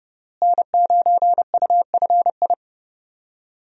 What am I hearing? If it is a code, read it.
Morse code, heard as a keyed tone, about 30 wpm, 700 Hz. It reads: N9UFI
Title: USA Call Signs - 5 Characters - Repeated 3x - 30wpm